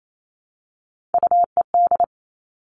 Morse code, heard as a keyed tone, about 28 wpm, 700 Hz. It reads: UED
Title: Three Challenging Letter-Number Combinations 28wpm